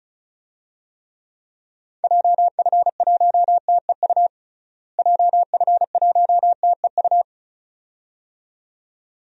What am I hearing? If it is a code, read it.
Morse code, heard as a keyed tone, about 36 wpm, 700 Hz. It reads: JF1TEU JF1TEU
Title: DXCC-100 Call Signs - 6 Characters - Repeated 2x - 35wpm